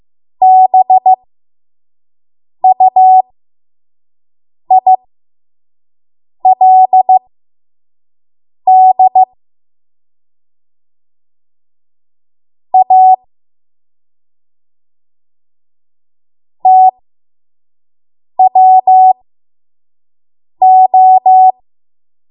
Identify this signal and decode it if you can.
Morse code, heard as a keyed tone, about 15 wpm, spaced as for 5 wpm, 750 Hz. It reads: BUILD A TWO
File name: Morse / 05WPM